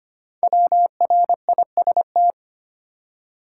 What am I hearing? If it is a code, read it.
Morse code, heard as a keyed tone, about 25 wpm, 700 Hz. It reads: WRIST
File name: 5 Letter Words - First 3 Letters Given - 25wpm